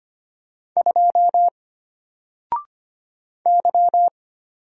Morse code, heard as a keyed tone, about 25 wpm, 700 Hz. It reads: 2 Y
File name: Single-Letter-Number-25wpm